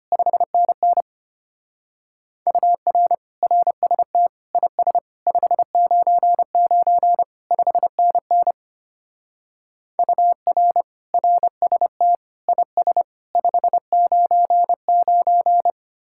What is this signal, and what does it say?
Morse code, heard as a keyed tone, about 30 wpm, 700 Hz. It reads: 5NN UR RST IS 599 5NN UR RST IS 599
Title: Speed-Racing Common QSO Phrases - 25wpm